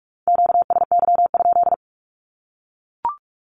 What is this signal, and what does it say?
Morse code, heard as a keyed tone, about 44 wpm, 700 Hz. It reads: QSY?
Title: Q-Codes 45wpm